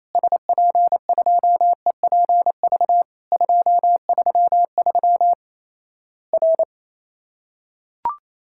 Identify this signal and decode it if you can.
Morse code, heard as a keyed tone, about 28 wpm, 700 Hz. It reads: SP2EPV 233 R
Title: Worked All Europe DX Contest - QTC Exchange - 28wpm